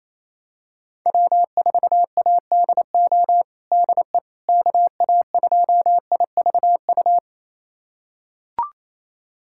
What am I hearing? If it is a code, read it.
Morse code, heard as a keyed tone, about 28 wpm, 700 Hz. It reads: W4ADO DE KA2IVU
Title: USA Callsign DE Callsign 28wpm